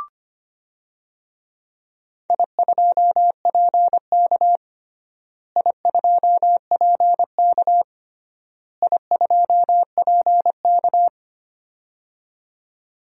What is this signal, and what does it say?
Morse code, heard as a keyed tone, about 25 wpm, 700 Hz. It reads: I2PK I2PK I2PK
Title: DXCC-100 Call Signs - 4 Characters - Repeated 3x - 25wpm